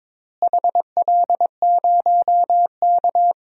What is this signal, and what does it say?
Morse code, heard as a keyed tone, about 21 wpm, 700 Hz. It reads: HL0K
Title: DXCC-100 Call Signs - 4 Characters - Repeated 3x - 22wpm